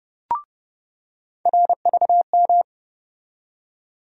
Morse code, heard as a keyed tone, about 30 wpm, 700 Hz. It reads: RVM
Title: USA Call Sign Postfixes - 3 Letters 30wpm